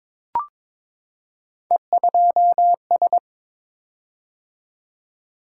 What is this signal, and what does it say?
Morse code, heard as a keyed tone, about 22 wpm, 700 Hz. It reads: E2S
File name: practice / 22wpm / Three Letter-Number Combinations - 22wpm